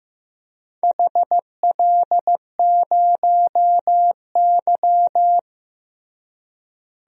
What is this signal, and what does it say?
Morse code, heard as a keyed tone, about 16 wpm, 700 Hz. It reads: HL0Y
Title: DXCC-100 Call Signs - 4 Characters - Repeated 2x - 15wpm